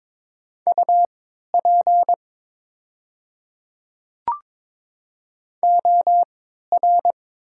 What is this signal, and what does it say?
Morse code, heard as a keyed tone, about 22 wpm, spaced as for 13 wpm, 690 Hz. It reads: UP OR
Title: 3x Character Spacing - Top 300 Words - 22wpm